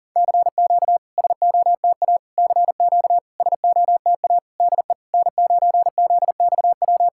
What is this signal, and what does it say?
Morse code, heard as a keyed tone, about 40 wpm, 690 Hz. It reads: CQ SOTA CQ SOTA DE N9ZXW
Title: SOTA QSO Practice 40wpm